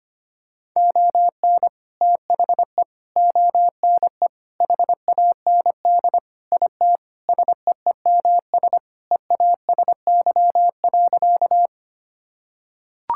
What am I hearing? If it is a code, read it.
Morse code, heard as a keyed tone, about 25 wpm, 700 Hz. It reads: ON THE ONE HAND IT SEEMS EASY.
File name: practice / 25wpm / Sentences with 4-Word Phrases - 25wpm